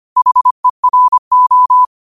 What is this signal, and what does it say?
Morse code, heard as a keyed tone, about 25 wpm, 1000 Hz. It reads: HERO
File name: qotdfiles / 25wpm